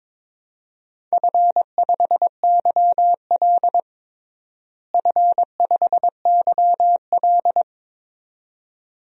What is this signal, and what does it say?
Morse code, heard as a keyed tone, about 22 wpm, 700 Hz. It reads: F5YL F5YL
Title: DXCC-100 Call Signs - 4 Characters - Repeated 2x - 22wpm